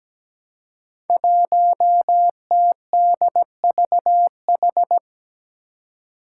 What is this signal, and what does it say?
Morse code, heard as a keyed tone, about 17 wpm, 700 Hz. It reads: 1TDVH
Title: Five Challenging Letter-Number Combinations 17wpm